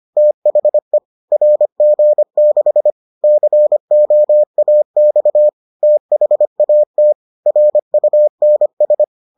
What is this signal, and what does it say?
Morse code, heard as a keyed tone, about 25 wpm, 600 Hz. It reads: THE RG6 COAX THAT RUNS